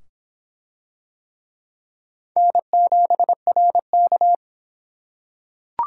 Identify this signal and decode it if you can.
Morse code, heard as a keyed tone, about 22 wpm, 700 Hz. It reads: N7RK